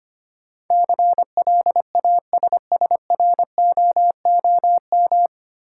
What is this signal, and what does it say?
Morse code, heard as a keyed tone, about 25 wpm, 700 Hz. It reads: CLASSROOM
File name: Nine Letter Words - 25wpm